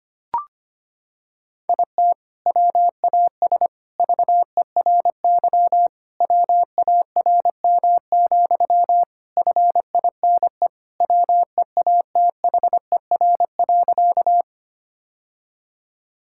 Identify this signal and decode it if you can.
Morse code, heard as a keyed tone, about 25 wpm, 700 Hz. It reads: IT WAS VERY WARM, FINE WEATHER.
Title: Sentences from Top 1000 Words - Review Part 5 - 25wpm